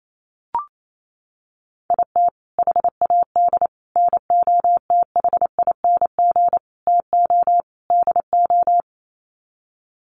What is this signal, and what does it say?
Morse code, heard as a keyed tone, about 28 wpm, 700 Hz. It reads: IT HAD NOTHING TO DO
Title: Sets of 5 Words Encore2 28wpm